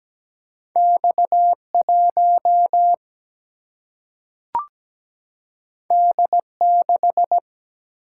Morse code, heard as a keyed tone, about 17 wpm, 700 Hz. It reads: X 1 D 6
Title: Single-Letter Single-Number 17wpm